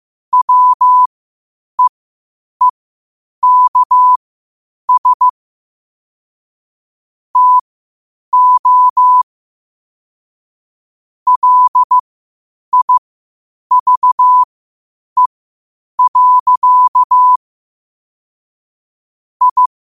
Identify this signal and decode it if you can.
Morse code, heard as a keyed tone, about 15 wpm, spaced as for 8 wpm, 1000 Hz. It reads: WEEKS TO LIVE. I
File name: 8wpm